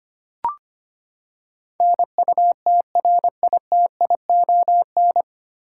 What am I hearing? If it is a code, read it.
Morse code, heard as a keyed tone, about 25 wpm, 700 Hz. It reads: NUTRITION
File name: Nine Letter Words - 25wpm